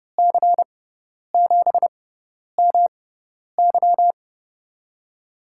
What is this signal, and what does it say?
Morse code, heard as a keyed tone, about 30 wpm, spaced as for 10 wpm, 700 Hz. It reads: C7MY
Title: USA Call Signs - 6x Character Spacing - 30wpm